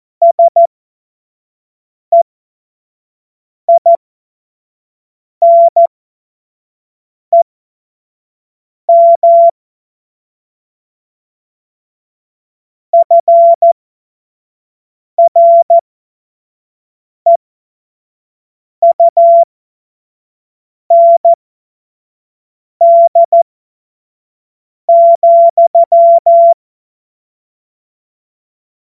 Morse code, heard as a keyed tone, about 14 wpm, spaced as for 5 wpm, 680 Hz. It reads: SEINEM FREUND,
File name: Morsekurs / Witze / Witz 04. Juli 2023 25 F70